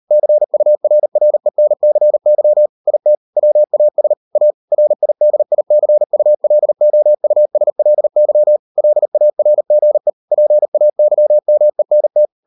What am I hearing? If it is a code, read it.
Morse code, heard as a keyed tone, about 38 wpm, 600 Hz. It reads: CURRENCY IT WAS A RIDICULOUSLY LARGE PAYMENT